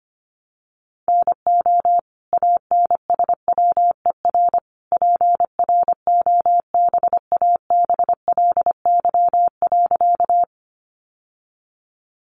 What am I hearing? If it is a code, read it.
Morse code, heard as a keyed tone, about 25 wpm, 700 Hz. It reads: NO ANSWER PROBABLY.
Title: Sentences from Top 700 Words 25wpm